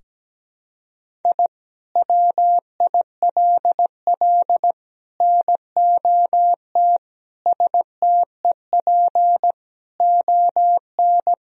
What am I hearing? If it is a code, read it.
Morse code, heard as a keyed tone, about 17 wpm, 700 Hz. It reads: I WILL NOT STEP ON
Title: Sentences from Top 1000 Words - Review Part 2 - 17wpm